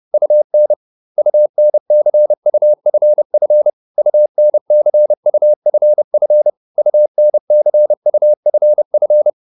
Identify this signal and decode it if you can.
Morse code, heard as a keyed tone, about 30 wpm, 600 Hz. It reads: UN UNCUFF UNCUFF UNCUFF
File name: UN6 30